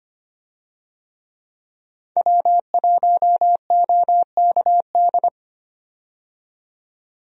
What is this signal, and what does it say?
Morse code, heard as a keyed tone, about 25 wpm, 700 Hz. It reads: W1OKD
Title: USA Call Signs - 5 Characters - 25wpm